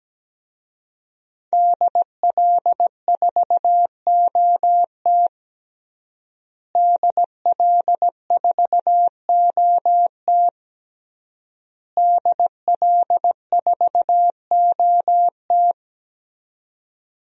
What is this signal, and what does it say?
Morse code, heard as a keyed tone, about 17 wpm, 700 Hz. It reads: DL4OT DL4OT DL4OT
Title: USA Call Signs - 5 Characters - Repeated 3x - 17wpm